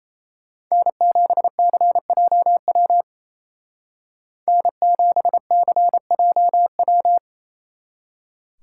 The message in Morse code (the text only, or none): N7CJW N7CJW